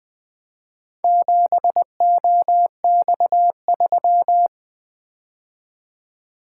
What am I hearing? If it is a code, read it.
Morse code, heard as a keyed tone, about 20 wpm, 700 Hz. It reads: 7OX3
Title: Four Letter-Number Combinations - 20wpm